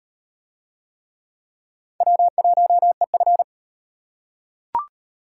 Morse code, heard as a keyed tone, about 36 wpm, 700 Hz. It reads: W1EF